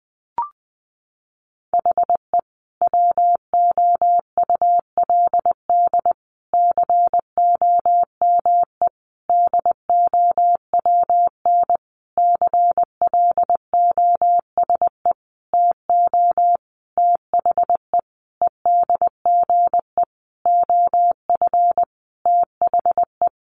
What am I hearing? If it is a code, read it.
Morse code, heard as a keyed tone, about 20 wpm, 700 Hz. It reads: HE WOULD COME DOWN CLOSE TO THE EDGE OF THE
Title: Sentences from Top 1300 Words 20wpm